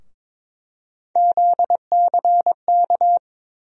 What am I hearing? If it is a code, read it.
Morse code, heard as a keyed tone, about 23 wpm, 700 Hz. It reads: ZCK
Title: USA Call Sign Postfixes - 3 Letters 22wpm